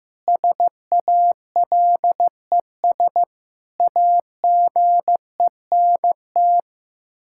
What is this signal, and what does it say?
Morse code, heard as a keyed tone, about 15 wpm, 700 Hz. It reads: SALES AGENT
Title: K1USN SST QSO Practice 15wpm